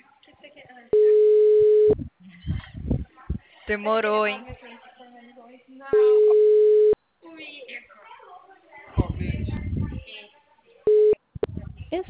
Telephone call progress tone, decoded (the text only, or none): ringback tone